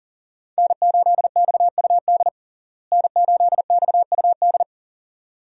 Morse code, heard as a keyed tone, about 40 wpm, 700 Hz. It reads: N8XUD N8XUD
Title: USA Call Signs - 5 Characters - Repeated 2x - 40wpm